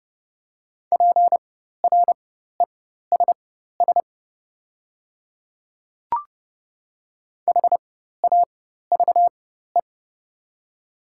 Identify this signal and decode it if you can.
Morse code, heard as a keyed tone, about 30 wpm, spaced as for 14 wpm, 700 Hz. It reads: PRESS HAVE
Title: 4x Character Spacing - Top 500 Words - 30wpm